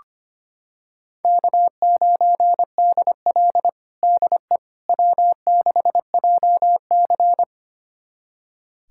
Morse code, heard as a keyed tone, about 25 wpm, 700 Hz. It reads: K9DL DE W6JC
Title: USA Callsign DE Callsign - 4 Characters - 25wpm